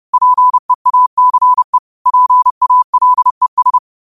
Morse code, heard as a keyed tone, about 30 wpm, 1000 Hz. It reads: PEACE PALES
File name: qotdfiles / 30wpm